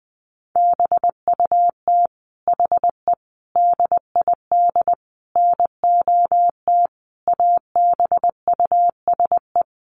Morse code, heard as a keyed tone, about 20 wpm, 700 Hz. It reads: BUT HE DID NOT ABUSE
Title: Sentences from Top 1300 Words 20wpm